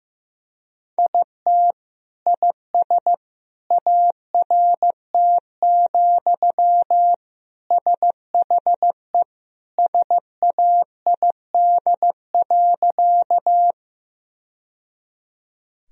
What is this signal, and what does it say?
Morse code, heard as a keyed tone, about 15 wpm, 700 Hz. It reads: IT IS ART, SHE SAID.
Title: Sentences from Top 1000 Words - Review Part 3 - 15wpm